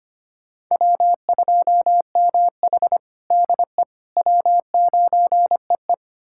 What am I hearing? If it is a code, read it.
Morse code, heard as a keyed tone, about 25 wpm, 700 Hz. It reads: W2MH DE W9EE
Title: USA Callsign DE Callsign - 4 Characters - 25wpm